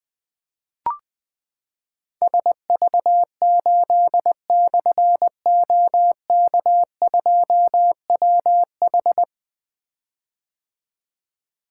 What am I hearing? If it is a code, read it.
Morse code, heard as a keyed tone, about 20 wpm, 700 Hz. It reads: SV8/OK2WH
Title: Difficult Call Signs 20wpm